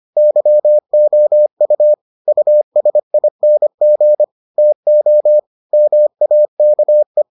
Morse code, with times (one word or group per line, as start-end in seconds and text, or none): 0.2-1.9 YOU
2.3-4.2 USING
4.6-5.4 TO
5.7-7.2 MAKE